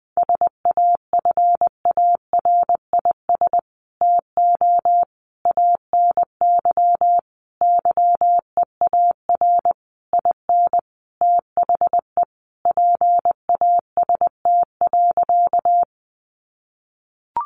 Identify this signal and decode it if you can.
Morse code, heard as a keyed tone, about 20 wpm, 700 Hz. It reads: SAFARIS TO ANY YEAR IN THE PAST.
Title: A Sound Of Thunder 20wpm